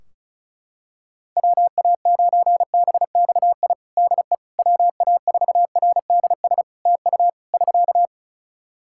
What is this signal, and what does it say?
Morse code, heard as a keyed tone, about 34 wpm, 700 Hz. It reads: WA9BXI DE WA4RDS TU <SK>
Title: Practice QSO - Exchange 4 - Ending - 35wpm